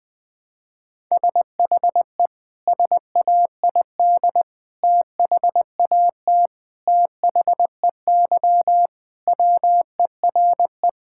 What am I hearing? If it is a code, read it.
Morse code, heard as a keyed tone, about 20 wpm, 700 Hz. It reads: SHE SAID THAT THEY WERE
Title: Sentences with 3-Word Phrases - 20wpm